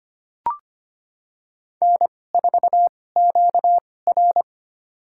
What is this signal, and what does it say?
Morse code, heard as a keyed tone, about 25 wpm, spaced as for 18 wpm, 700 Hz. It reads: N4QR
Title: USA Call Signs - 4 Characters - 2x Character Spacing - 25wpm